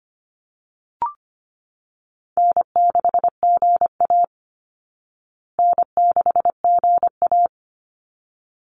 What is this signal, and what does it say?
Morse code, heard as a keyed tone, about 25 wpm, 700 Hz. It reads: N6GA N6GA
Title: Easy Call Signs Repeated 2x - 25wpm